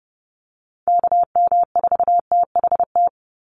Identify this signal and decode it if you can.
Morse code, heard as a keyed tone, about 30 wpm, 700 Hz. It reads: KM4THT
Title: USA Call Signs 30wpm